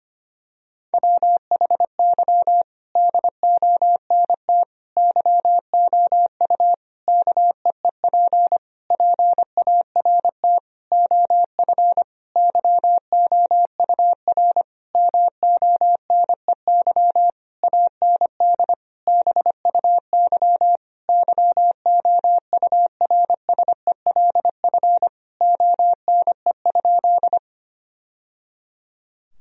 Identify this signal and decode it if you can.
Morse code, heard as a keyed tone, about 25 wpm, 700 Hz. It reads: WHY DONT YOU KEEP PART OF YOUR MONEY AND BUY YOURSELF ONE?
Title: Sentences from Top 1000 Words - Review Part 4 - 25wpm